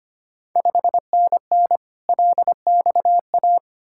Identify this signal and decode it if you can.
Morse code, heard as a keyed tone, about 25 wpm, 700 Hz. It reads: 5NN LXA
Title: Ontario QSO Party Exchange 25wpm